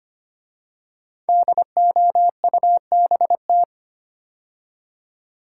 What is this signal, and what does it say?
Morse code, heard as a keyed tone, about 25 wpm, 700 Hz. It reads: DOUBT
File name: 5 Letter Words - First 3 Letters Given - 25wpm